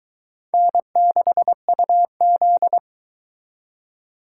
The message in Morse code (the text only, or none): N6UZ